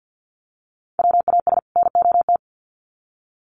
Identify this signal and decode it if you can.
Morse code, heard as a keyed tone, about 50 wpm, 700 Hz. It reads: WAS NOT
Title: Sets of 2 Words 50wpm